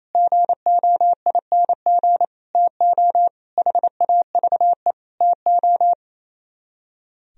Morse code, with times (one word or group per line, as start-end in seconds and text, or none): 0.2-2.2 GOING
2.6-3.3 TO
3.6-4.9 HAVE
5.2-5.9 TO